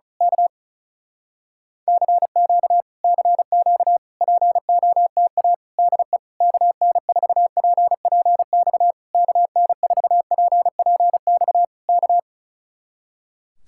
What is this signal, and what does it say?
Morse code, heard as a keyed tone, about 33 wpm, 700 Hz. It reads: K CQ CQ POTA DE KN4PPX KN4PPX K
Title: Speed-Racing Common QSO Phrases - 35wpm